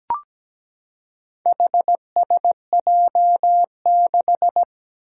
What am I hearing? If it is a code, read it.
Morse code, heard as a keyed tone, about 17 wpm, 700 Hz. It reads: HSJ6
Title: Four Challenging Letter-Number Combinations 17wpm